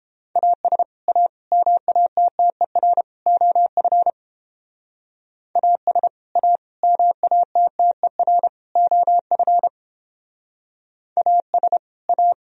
Morse code, heard as a keyed tone, about 29 wpm, 700 Hz. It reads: AS A MATTER OF AS A MATTER OF AS A